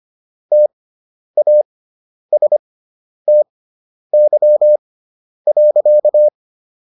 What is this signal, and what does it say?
Morse code, heard as a keyed tone, about 25 wpm, spaced as for 10 wpm, 600 Hz. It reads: TASTY.